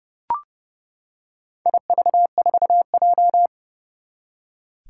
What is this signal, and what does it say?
Morse code, heard as a keyed tone, about 30 wpm, 700 Hz. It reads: IV4J
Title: Four Challenging Letter-Number Combinations 30wpm